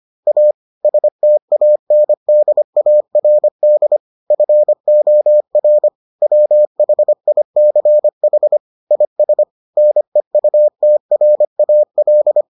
Morse code, heard as a keyed tone, about 25 wpm, 600 Hz. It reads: A STANDARD FOR WHICH IS NEUTRAL